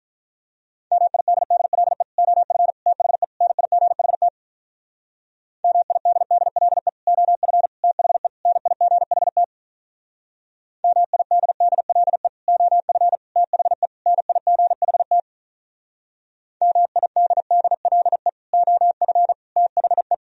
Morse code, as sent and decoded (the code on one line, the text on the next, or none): -- .. -.. -.. .-.. / --- ..-. / - .... / -. .. --. .... - / -- .. -.. -.. .-.. / --- ..-. / - .... / -. .. --. .... - / -- .. -.. -.. .-.. / --- ..-. / - .... / -. .. --. .... - / -- .. -.. -.. .-.. / --- ..-. / - .... .
MIDDLE OF THE NIGHT MIDDLE OF THE NIGHT MIDDLE OF THE NIGHT MIDDLE OF THE